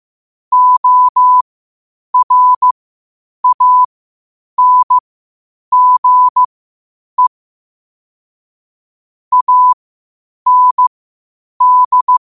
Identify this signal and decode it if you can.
Morse code, heard as a keyed tone, about 15 wpm, spaced as for 8 wpm, 1000 Hz. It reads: ORANGE AND
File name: qotdfiles / 8wpm